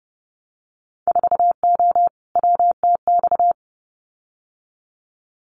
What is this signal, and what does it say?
Morse code, heard as a keyed tone, about 30 wpm, 700 Hz. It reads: 4O WTX
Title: Winter Field Day Exchanges Encore 30wpm